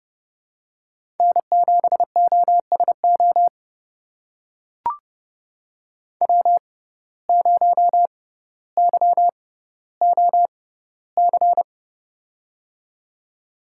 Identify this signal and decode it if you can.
Morse code, heard as a keyed tone, about 30 wpm, spaced as for 15 wpm, 700 Hz. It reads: N7OSO W0YOC
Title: USA Call Signs - 5 Characters - 6x Character Spacing - 30wpm